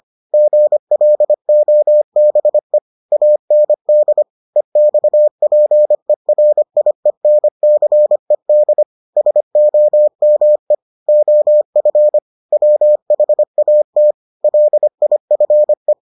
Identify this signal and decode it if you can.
Morse code, heard as a keyed tone, about 25 wpm, 600 Hz. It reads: GLOBE AND EXPERIENCED SOME OF WHAT LIFE